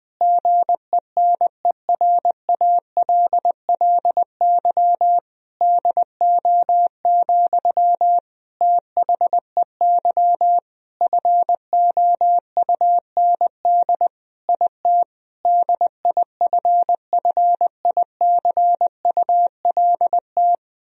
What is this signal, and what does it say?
Morse code, heard as a keyed tone, about 20 wpm, 700 Hz. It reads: GENERALLY DO, THEY FOUND IT DIFFICULT
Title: Sentences from Top 1000 Words - Review Part 4 - 20wpm